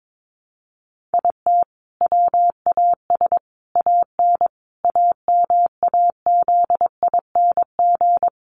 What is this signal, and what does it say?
Morse code, heard as a keyed tone, about 22 wpm, 700 Hz. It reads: IT WAS AN AMAZING
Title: Sentences from Top 1500 Words 22wpm